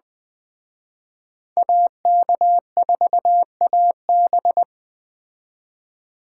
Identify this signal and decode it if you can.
Morse code, heard as a keyed tone, about 20 wpm, 700 Hz. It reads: AK4AB